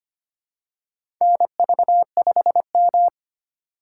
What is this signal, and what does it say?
Morse code, heard as a keyed tone, about 25 wpm, 700 Hz. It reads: NV5M
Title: USA Call Signs - 4 Characters - 4x Character Spacing - 25wpm